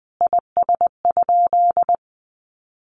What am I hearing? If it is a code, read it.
Morse code, heard as a keyed tone, about 20 wpm, 700 Hz. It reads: IS?
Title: Sentences from Top 1500 Words 20wpm